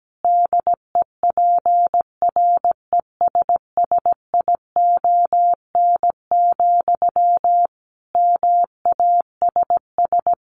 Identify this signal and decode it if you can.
Morse code, heard as a keyed tone, about 17 wpm, 700 Hz. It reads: DEPRESSION, MASS
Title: A Sound Of Thunder 17wpm